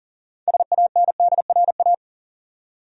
Morse code, heard as a keyed tone, about 40 wpm, 700 Hz. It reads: SANDRA
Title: USA Common Names - 40wpm